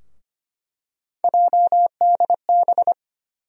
Morse code, heard as a keyed tone, about 25 wpm, 700 Hz. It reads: JDB
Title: USA Call Sign Postfixes - 3 Letters 25wpm